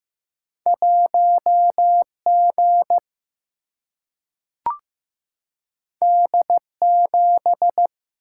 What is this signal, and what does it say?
Morse code, heard as a keyed tone, about 15 wpm, 700 Hz. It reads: 1 G D 7
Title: Single-Letter Single-Number 15wpm